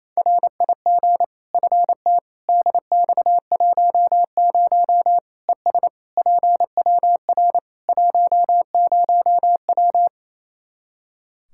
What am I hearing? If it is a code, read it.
Morse code, heard as a keyed tone, about 28 wpm, 700 Hz. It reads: RIG FT DX10 ES PWR 10W
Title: Practice QSO - Exchange 2 - Info - 28wpm